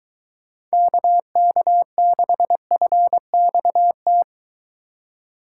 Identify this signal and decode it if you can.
Morse code, heard as a keyed tone, about 22 wpm, 700 Hz. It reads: KK6FXT